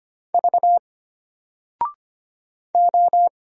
Morse code, heard as a keyed tone, about 25 wpm, 700 Hz. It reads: V O
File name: Single-Letter-Number-25wpm